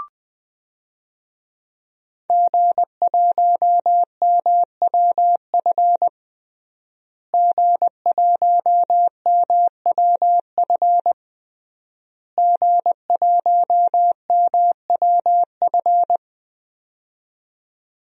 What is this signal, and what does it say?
Morse code, heard as a keyed tone, about 20 wpm, 700 Hz. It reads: G1MWF G1MWF G1MWF
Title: DXCC-100 Call Signs - 5 Characters - Repeated 3x - 20wpm